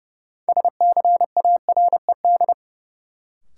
Morse code, heard as a keyed tone, about 29 wpm, 700 Hz. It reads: SCARED
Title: Sentences from Top 700 Words - New Word Drill - 30wpm